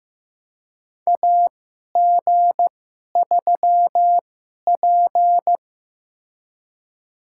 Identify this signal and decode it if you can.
Morse code, heard as a keyed tone, about 15 wpm, spaced as for 11 wpm, 700 Hz. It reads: AG3P
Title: USA Call Signs - 4 Characters - 2x Character Spacing - 15wpm